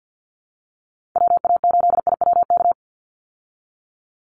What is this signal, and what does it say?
Morse code, heard as a keyed tone, about 50 wpm, 700 Hz. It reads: WA8IWK
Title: Call Signs Repeated 2x - 50wpm